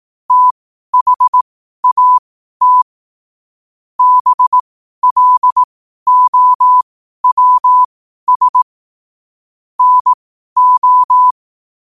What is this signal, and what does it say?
Morse code, heard as a keyed tone, about 18 wpm, spaced as for 12 wpm, 1000 Hz. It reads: THAT BLOWS NO
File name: qotdfiles / 12wpm